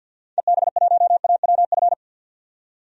Morse code, heard as a keyed tone, about 53 wpm, 700 Hz. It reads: EB1AWF
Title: DXCC-100 Call Signs - 6 Characters - 50wpm